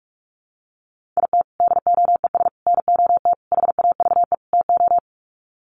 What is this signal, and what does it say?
Morse code, heard as a keyed tone, about 44 wpm, 700 Hz. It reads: IT DOES NOT HAVE TO
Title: Sets of 5 Words 45wpm